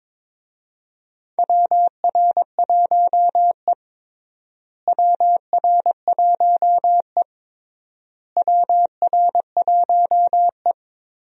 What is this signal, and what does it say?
Morse code, heard as a keyed tone, about 22 wpm, 700 Hz. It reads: WR1E WR1E WR1E
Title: USA Call Signs - 4 Characters - Repeated 3x - 22wpm